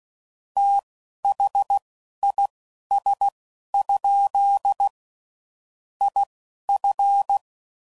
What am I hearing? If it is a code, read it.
Morse code, heard as a keyed tone, about 16 wpm, spaced as for 11 wpm, 800 Hz. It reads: THIS? IF
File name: CW Code Groups @10WPM/Technical Subjects